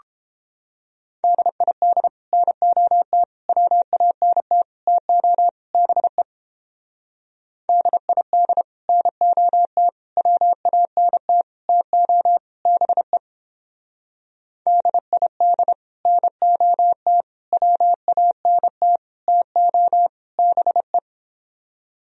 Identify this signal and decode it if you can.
Morse code, heard as a keyed tone, about 29 wpm, 700 Hz. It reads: DID NOT WANT TO BE DID NOT WANT TO BE DID NOT WANT TO BE